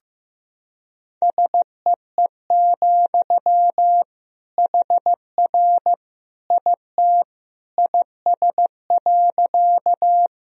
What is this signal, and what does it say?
Morse code, heard as a keyed tone, about 15 wpm, 700 Hz. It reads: SEE, HR IT IS.
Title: Sentences from Top 200 Words and Abbreviations 15wpm